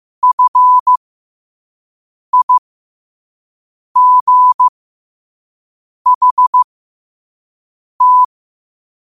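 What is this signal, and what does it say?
Morse code, heard as a keyed tone, about 15 wpm, spaced as for 5 wpm, 1000 Hz. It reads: FIGHT
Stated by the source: Morse code sent at 5 wpm.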